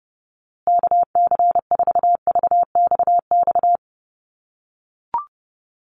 Morse code, heard as a keyed tone, about 30 wpm, 700 Hz. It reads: KC4VXX
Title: USA Call Signs Repeated 2x 30wpm